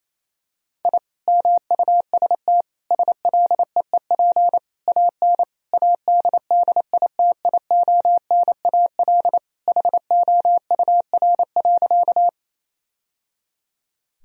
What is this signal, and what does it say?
Morse code, heard as a keyed tone, about 28 wpm, 700 Hz. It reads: I MUST SLEEP AN ADDITIONAL HOUR.
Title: Sentences from Top 1000 Words - Review Part 4 - 28wpm